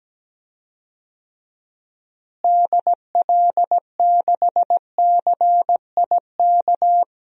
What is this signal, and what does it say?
Morse code, heard as a keyed tone, about 17 wpm, 700 Hz. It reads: DL6CIK
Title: DXCC-100 Call Signs - 6 Characters - Repeated 3x - 17wpm